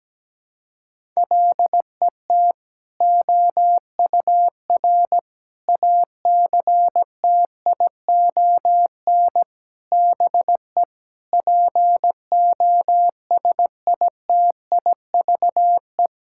Morse code, heard as a keyed tone, about 17 wpm, 700 Hz. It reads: LET OUR ACTION BE POSITIVE
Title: Sentences from Top 1000 Words - Review Part 6 - 17wpm